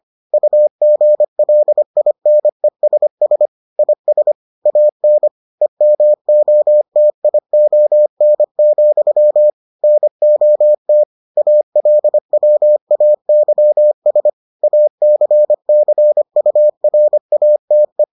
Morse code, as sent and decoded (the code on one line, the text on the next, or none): ..- --. .-.. .. -. ... ... / .. ... / .- -. / . -- --- - .. --- -. --..-- / -. --- - / .- .-.. .-- .- -.-- ... / .- -.-. -.-. ..- .-. .- - .
UGLINESS IS AN EMOTION, NOT ALWAYS ACCURATE